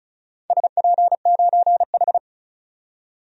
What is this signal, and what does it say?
Morse code, heard as a keyed tone, about 34 wpm, 700 Hz. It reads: SP9H
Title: DXCC-100 Call Signs - 4 Characters - 35wpm